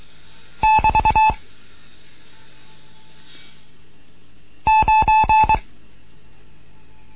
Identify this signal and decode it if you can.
Morse code, heard as a keyed tone, about 22 wpm, 870 Hz. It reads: <BT> 9